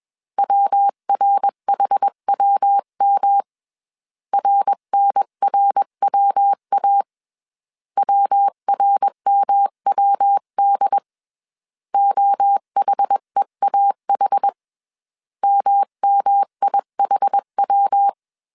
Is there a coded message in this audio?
Yes: Morse code from 0.4 to 18.1 s, heard as a keyed tone, about 21 wpm, 800 Hz. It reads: WRHWM RNRWA WRMWD OHEAH MMIHW